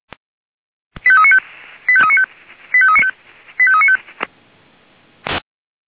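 Five-tone selective call (ZVEI), heard as quick runs of five tones, tones of about 70 ms. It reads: 85395 85395 85395
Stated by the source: ZVEI five-tone selective call 85383, received on an SDR.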